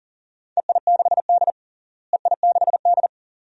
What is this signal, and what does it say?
Morse code, heard as a keyed tone, about 40 wpm, 700 Hz. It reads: EI6D EI6D
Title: DXCC-100 Call Signs - 4 Characters - Repeated 2x - 40wpm